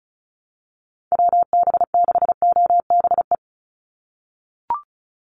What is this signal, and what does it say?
Morse code, heard as a keyed tone, about 34 wpm, 700 Hz. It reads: WB6OBE
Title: USA Call Signs 35wpm